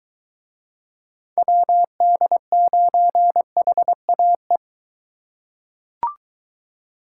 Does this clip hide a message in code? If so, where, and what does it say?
Yes: Morse code from 1.4 to 4.6 s, heard as a keyed tone, about 23 wpm, 700 Hz. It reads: WD9HAE